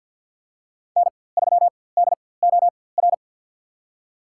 Morse code, heard as a keyed tone, about 50 wpm, spaced as for 23 wpm, 700 Hz. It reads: N3DOR
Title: USA Call Signs - 5 Characters - 4x Character Spacing - 50wpm